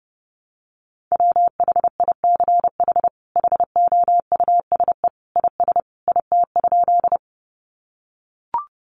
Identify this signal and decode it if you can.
Morse code, heard as a keyed tone, about 30 wpm, 700 Hz. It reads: WHICH HOUSE IS IT?
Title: Sentences from Top 200 Words 30wpm